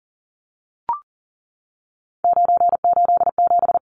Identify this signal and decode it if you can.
Morse code, heard as a keyed tone, about 40 wpm, 700 Hz. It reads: 987
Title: Numbers - 3-Digit 40wpm